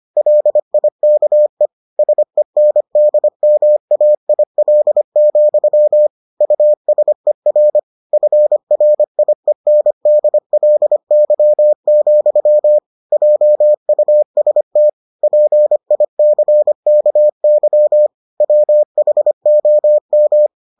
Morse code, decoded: LIKE SENDMAIL, USER FRIENDLY, JUST PICKY WHOM

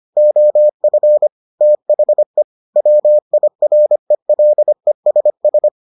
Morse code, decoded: OF THE WIRELESS

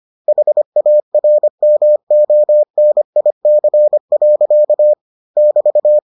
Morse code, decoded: HARMONIC. <BT>

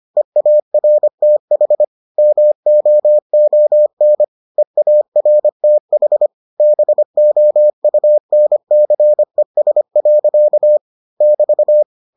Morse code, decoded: EARTH MOON EARTH BOUNCES. <BT>

